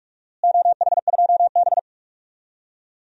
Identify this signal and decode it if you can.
Morse code, heard as a keyed tone, about 46 wpm, 700 Hz. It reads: OH2B
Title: DXCC-100 Call Signs - 4 Characters - Repeated 2x - 45wpm